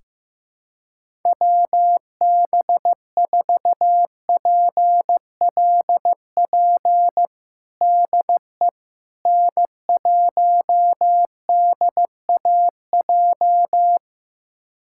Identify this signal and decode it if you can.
Morse code, heard as a keyed tone, about 15 wpm, 700 Hz. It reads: WB4PLP DE N1DAJ
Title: USA Callsign DE Callsign 15wpm